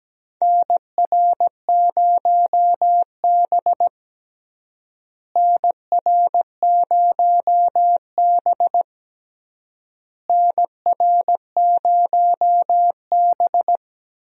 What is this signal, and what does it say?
Morse code, heard as a keyed tone, about 17 wpm, 700 Hz. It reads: NR0B NR0B NR0B
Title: USA Call Signs - 4 Characters - Repeated 3x - 17wpm